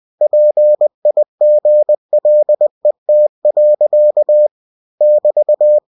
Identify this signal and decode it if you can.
Morse code, heard as a keyed tone, about 20 wpm, 600 Hz. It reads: PIGLET. <BT>
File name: cw-books / winnie-the-pooh / winnie-20wpm-page-0080